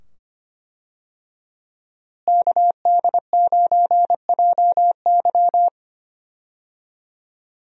Morse code, heard as a keyed tone, about 25 wpm, 700 Hz. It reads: KD9JY
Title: USA Call Signs - 5 Characters - Repeated 2x - 25wpm